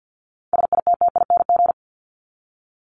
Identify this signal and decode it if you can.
Morse code, heard as a keyed tone, about 50 wpm, 700 Hz. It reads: SITTING
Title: ING Suffix - Top 100 Words with Base Words 50wpm